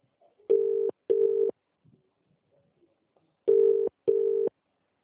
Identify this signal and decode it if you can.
Ringback tone (UK), 400 + 450 Hz.